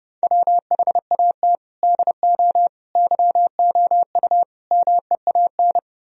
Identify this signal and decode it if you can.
Morse code, heard as a keyed tone, about 40 wpm, 700 Hz. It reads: WHAT DO YOU MEAN